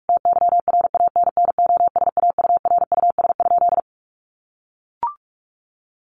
Morse code, heard as a keyed tone, about 44 wpm, 700 Hz. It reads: TYRANNOSAURUS?
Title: A Sound Of Thunder 45wpm